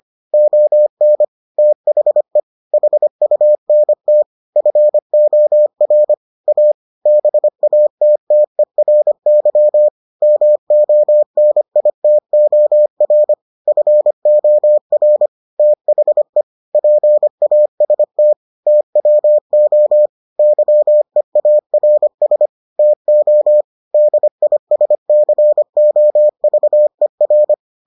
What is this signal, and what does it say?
Morse code, heard as a keyed tone, about 25 wpm, 600 Hz. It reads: ON THE HUNT FOR A BATTERY MONITOR FOR THE PAST TWO YEARS TO DISCOVER